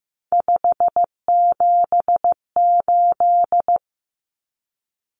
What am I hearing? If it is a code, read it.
Morse code, heard as a keyed tone, about 15 wpm, 700 Hz. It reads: 578
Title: Numbers - 3-Digit 15wpm